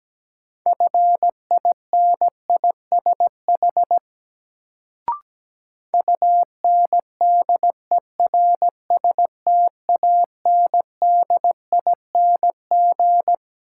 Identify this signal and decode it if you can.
Morse code, heard as a keyed tone, about 17 wpm, 700 Hz. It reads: FINISH UNDERSTANDING
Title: Sentences from Top 1000 Words - Review Part 3 - New Word Drill - 17wpm